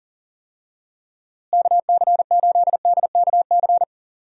Morse code, heard as a keyed tone, about 38 wpm, 700 Hz. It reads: KC8DKC
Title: USA Call Signs - 6 Characters - Repeated 3x - 40wpm